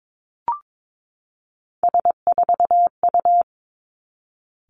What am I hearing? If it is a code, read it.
Morse code, heard as a keyed tone, about 22 wpm, 700 Hz. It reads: S4U
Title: Three Challenging Letter-Number Combinations 22wpm